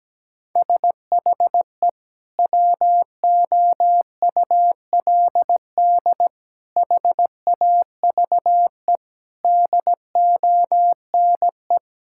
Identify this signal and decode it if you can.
Morse code, heard as a keyed tone, about 17 wpm, 700 Hz. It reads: SHE WOULD HAVE DONE
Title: Sentences from Top 1600 Words 17wpm